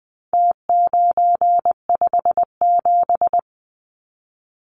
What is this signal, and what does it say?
Morse code, heard as a keyed tone, about 20 wpm, 700 Hz. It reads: T957
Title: Numbers - Contest Style - 4-Digit 20wpm